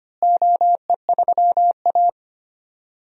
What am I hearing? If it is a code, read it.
Morse code, heard as a keyed tone, about 25 wpm, 700 Hz. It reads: OE3A
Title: DXCC-100 Call Signs - 4 Characters - Repeated 2x - 25wpm